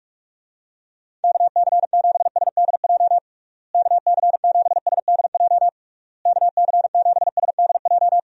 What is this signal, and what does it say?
Morse code, heard as a keyed tone, about 46 wpm, 700 Hz. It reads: KC7SDJ KC7SDJ KC7SDJ
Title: USA Call Signs - 6 Characters - Repeated 3x - 45wpm